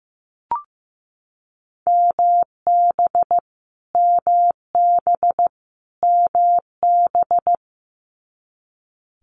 Morse code, heard as a keyed tone, about 15 wpm, 700 Hz. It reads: MB MB MB
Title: Multi-letter Phonemes Repeated 3x 15wpm